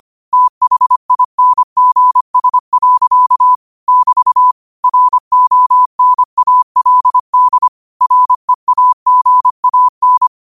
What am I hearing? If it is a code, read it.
Morse code, heard as a keyed tone, about 25 wpm, 1000 Hz. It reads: THINGS. <BT> RONALD REAGAN
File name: qotdfiles / 25wpm